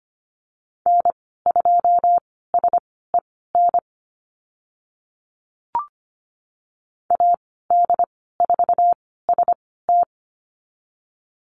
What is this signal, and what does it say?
Morse code, heard as a keyed tone, about 25 wpm, spaced as for 16 wpm, 690 Hz. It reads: N2SEN AD4ST
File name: Lesson 15g - Number 5 - Call Signs 25wpm